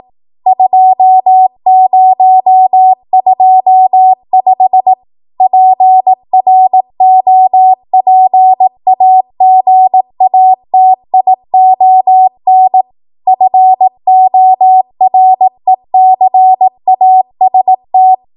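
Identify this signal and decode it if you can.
Morse code, heard as a keyed tone, about 18 wpm, 750 Hz. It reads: 2025 PROPAGATION FORECAST
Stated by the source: ARRL 18 wpm Morse code practice archive transmission.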